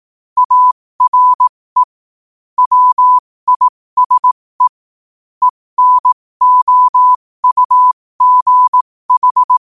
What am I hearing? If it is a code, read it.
Morse code, heard as a keyed tone, about 18 wpm, spaced as for 15 wpm, 1000 Hz. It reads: ARE WISE ENOUGH